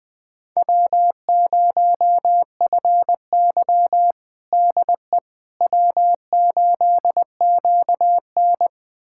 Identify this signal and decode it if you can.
Morse code, heard as a keyed tone, about 20 wpm, 700 Hz. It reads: W0FY DE W8QN
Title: USA Callsign DE Callsign - 4 Characters - 20wpm